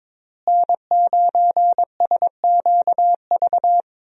Morse code, heard as a keyed tone, about 22 wpm, 700 Hz. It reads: N9SQV
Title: USA Call Signs - 5 Characters - Repeated 2x - 22wpm